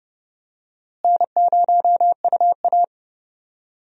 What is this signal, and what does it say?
Morse code, heard as a keyed tone, about 30 wpm, 700 Hz. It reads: N0UA
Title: USA Call Signs - 4 Characters - Repeated 3x - 30wpm